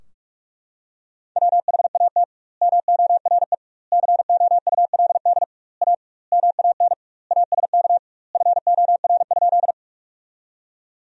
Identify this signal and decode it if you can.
Morse code, heard as a keyed tone, about 42 wpm, 700 Hz. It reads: WHAT MORE COULD A MAN ASK FOR?
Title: 2x Word-Spacing - Sentences from Top 200 Words - 45wpm